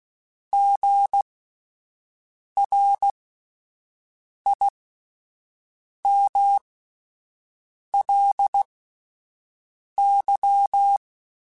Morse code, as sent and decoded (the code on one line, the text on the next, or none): --. .-. .. -- .-.. -.--
GRIMLY